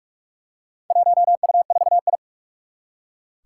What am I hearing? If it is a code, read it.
Morse code, heard as a keyed tone, about 44 wpm, 700 Hz. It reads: 1U4I
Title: Four Challenging Letter-Number Combinations 45wpm